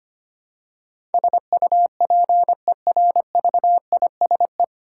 Morse code, heard as a keyed tone, about 25 wpm, 700 Hz. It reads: SUPERVISE
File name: practice / 25wpm / 9 Letter Words - First 3 Letters Given - 25wpm